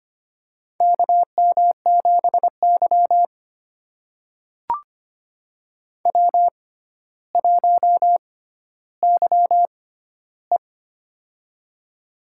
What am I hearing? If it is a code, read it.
Morse code, heard as a keyed tone, about 25 wpm, spaced as for 13 wpm, 700 Hz. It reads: KM7Y W1YE
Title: USA Call Signs - 4 Characters - 6x Character Spacing - 25wpm